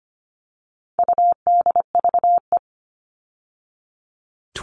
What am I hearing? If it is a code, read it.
Morse code, heard as a keyed tone, about 25 wpm, 700 Hz. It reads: UDVE
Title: All Cut Numbers - 4-Digit 25wpm